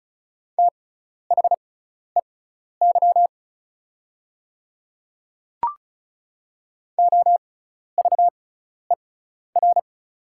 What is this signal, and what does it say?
Morse code, heard as a keyed tone, about 36 wpm, spaced as for 12 wpm, 700 Hz. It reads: THEY OVER
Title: 6x Character Spacing - Top 100 Words - 35wpm